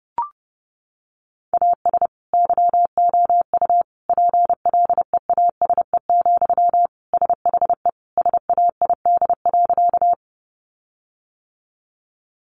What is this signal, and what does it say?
Morse code, heard as a keyed tone, about 30 wpm, 700 Hz. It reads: AS YOU PLEASE, SHE SAID.
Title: Sentences from Top 1500 Words 30wpm